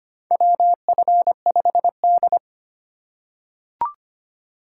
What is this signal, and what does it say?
Morse code, heard as a keyed tone, about 25 wpm, 700 Hz. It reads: WF5D
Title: USA Call Signs - 4 Characters - 2x Character Spacing - 25wpm